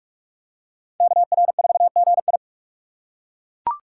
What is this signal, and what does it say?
Morse code, heard as a keyed tone, about 38 wpm, 700 Hz. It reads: KR4GI